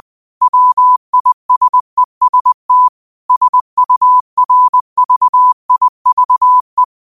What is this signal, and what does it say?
Morse code, heard as a keyed tone, about 20 wpm, 1000 Hz. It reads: WISEST SURVIVE
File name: qotdfiles / 20wpm